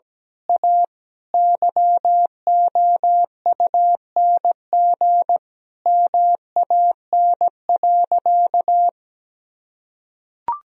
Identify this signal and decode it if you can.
Morse code, heard as a keyed tone, about 17 wpm, 700 Hz. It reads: A YOUNG MAN.
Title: Sentences from Top 1000 Words - Review Part 4 - 17wpm